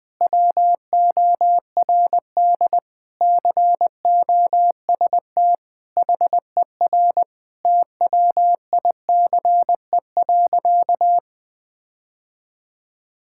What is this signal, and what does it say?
Morse code, heard as a keyed tone, about 21 wpm, 700 Hz. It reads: WORD COST HER TWICE.
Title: Sentences from Top 1600 Words 20wpm